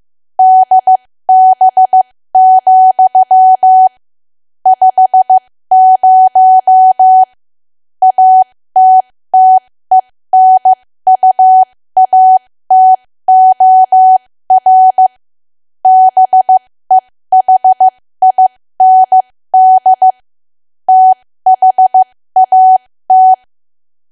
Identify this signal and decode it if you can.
Morse code, heard as a keyed tone, about 15 wpm, spaced as for 13 wpm, 750 Hz. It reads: DB, 50 ATTENUATOR BEHIND THAT
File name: cwa / W1AW-Level2 / 13WPM